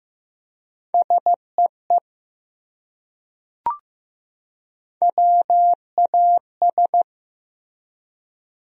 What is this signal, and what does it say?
Morse code, heard as a keyed tone, about 15 wpm, 700 Hz. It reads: SEE WAS
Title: Top 100 Words 15wpm